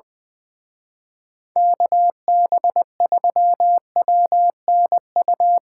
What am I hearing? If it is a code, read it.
Morse code, heard as a keyed tone, about 22 wpm, 700 Hz. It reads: KB3WNU
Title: Speed Racing - USA 2x3 Call Signs 20wpm